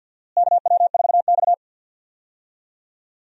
Morse code, heard as a keyed tone, about 50 wpm, 700 Hz. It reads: KW4X
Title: USA Call Signs - 4 Characters - Repeated 2x - 50wpm